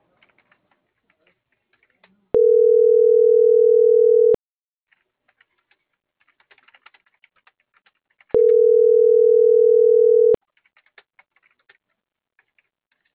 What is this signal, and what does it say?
Ringback tone (North America), 440 + 480 Hz.
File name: CENTRAL GOMSA / out francisco.cobos